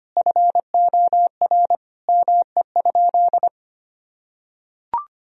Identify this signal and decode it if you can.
Morse code, heard as a keyed tone, about 25 wpm, 700 Hz. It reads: FOR ME?
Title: Sentences from Top 1000 Words - Review Part 3 - 25wpm